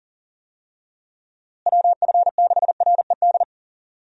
Morse code, heard as a keyed tone, about 40 wpm, 700 Hz. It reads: WF6RED
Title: USA Call Signs - 6 Characters - Repeated 3x - 40wpm